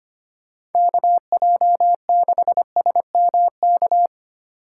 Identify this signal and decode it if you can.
Morse code, heard as a keyed tone, about 33 wpm, 700 Hz. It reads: KJ6SMK